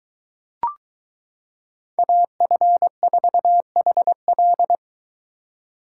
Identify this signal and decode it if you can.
Morse code, heard as a keyed tone, about 23 wpm, 700 Hz. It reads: AF4HL